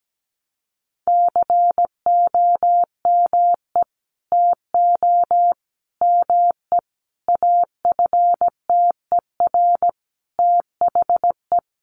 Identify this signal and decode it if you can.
Morse code, heard as a keyed tone, about 17 wpm, 700 Hz. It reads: COME TO ME AFTER THE
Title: Sentences from Top 900 Words 17wpm